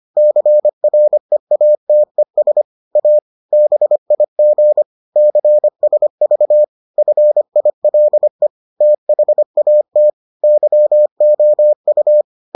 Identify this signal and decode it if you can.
Morse code, heard as a keyed tone, about 25 wpm, 600 Hz. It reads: CREATES A BIG CSV FILE THAT YOU